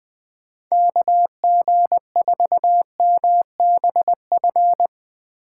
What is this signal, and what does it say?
Morse code, heard as a keyed tone, about 21 wpm, 700 Hz. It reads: KG4MBF
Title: Speed Racing - USA 2x3 Call Signs 20wpm